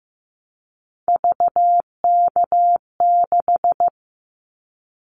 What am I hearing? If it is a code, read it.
Morse code, heard as a keyed tone, about 15 wpm, 700 Hz. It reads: VK6
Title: Callsign Prefixes for Top 100 DXCC - 15wpm